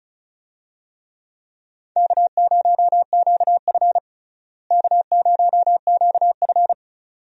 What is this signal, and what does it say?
Morse code, heard as a keyed tone, about 34 wpm, 700 Hz. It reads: K0QF K0QF
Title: USA Call Signs - 4 Characters - Repeated 2x - 35wpm